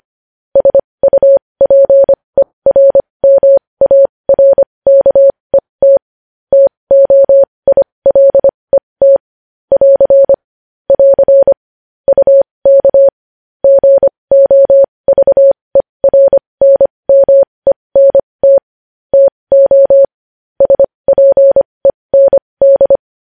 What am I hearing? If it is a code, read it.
Morse code, heard as a keyed tone, about 25 wpm, spaced as for 20 wpm, 550 Hz. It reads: SUPERMARKET TOILET <AR> <AR> UK GOVERNMENT TO SPEND
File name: podcast / News headlines-1x-25x20